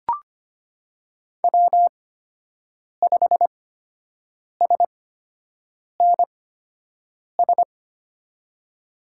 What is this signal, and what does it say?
Morse code, heard as a keyed tone, about 25 wpm, spaced as for 7 wpm, 700 Hz. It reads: W5SNS
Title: USA Call Signs - 8x Character Spacing - 25wpm